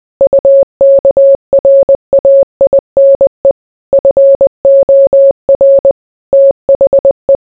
Morse code, heard as a keyed tone, about 20 wpm, 550 Hz. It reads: UKRAINE FOR THE